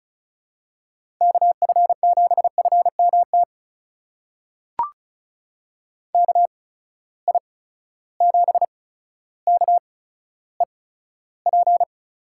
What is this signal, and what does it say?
Morse code, heard as a keyed tone, about 36 wpm, spaced as for 15 wpm, 700 Hz. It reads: KF7FMT KI7KEP
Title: USA Call Signs - 6 Characters - 8x Character Spacing - 35wpm